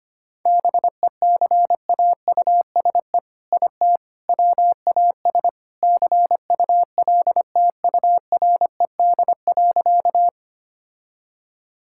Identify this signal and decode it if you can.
Morse code, heard as a keyed tone, about 25 wpm, 700 Hz. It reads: BECAUSE IT WAS CULTURED.
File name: practice / 25wpm / All-Age Appropriate Jokes - 25wpm